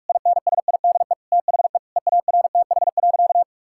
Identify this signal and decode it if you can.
Morse code, heard as a keyed tone, about 46 wpm, 700 Hz. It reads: INSIDE THE EARTH.
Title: Sentences from Top 1000 Words - Review Part 6 - 45wpm